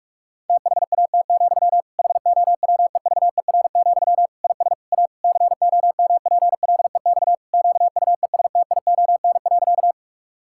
Morse code, heard as a keyed tone, about 46 wpm, 700 Hz. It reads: THAT, HOWEVER, IS A COMPLEX QUESTION.
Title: Sentences from Top 1000 Words - Review Part 6 - 45wpm